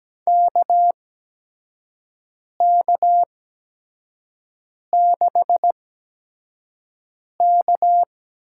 Morse code, heard as a keyed tone, about 17 wpm, spaced as for 5 wpm, 700 Hz. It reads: KK6K
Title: USA Call Signs - 6 Characters - 8x Character Spacing - 17wpm